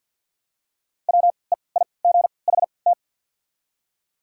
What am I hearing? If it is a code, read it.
Morse code, heard as a keyed tone, about 53 wpm, spaced as for 29 wpm, 700 Hz. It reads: WEIGHT
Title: 3x Character Spacing - Top 500 Words - 50wpm